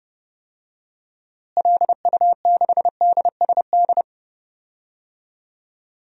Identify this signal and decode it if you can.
Morse code, heard as a keyed tone, about 30 wpm, 700 Hz. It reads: LU6DSD
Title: DXCC-100 Call Signs - 6 Characters - Repeated 3x - 30wpm